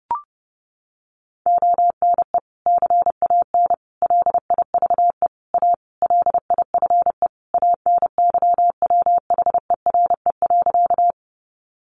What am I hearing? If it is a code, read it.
Morse code, heard as a keyed tone, about 30 wpm, 700 Hz. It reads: ONE CAN LIVE A LIFE ANYWHERE.
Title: Sentences from Top 1400 Words 30wpm